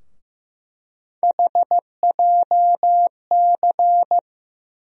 Morse code, heard as a keyed tone, about 15 wpm, 700 Hz. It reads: HJC
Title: USA Call Sign Postfixes - 3 Letters 15wpm